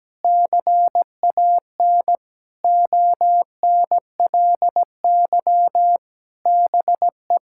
Morse code, heard as a keyed tone, about 17 wpm, 700 Hz. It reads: CAN ONLY BE
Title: Sentences from Top 1000 Words - Review Part 6 - 17wpm